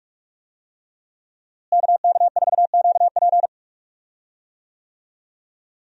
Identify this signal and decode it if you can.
Morse code, heard as a keyed tone, about 46 wpm, 700 Hz. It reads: KK4QP
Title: USA Call Signs - 5 Characters - Repeated 3x - 45wpm